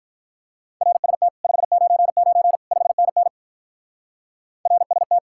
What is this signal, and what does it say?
Morse code, heard as a keyed tone, about 42 wpm, 700 Hz. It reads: RST 599 5NN RST
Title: Speed-Racing Common QSO Phrases - 35wpm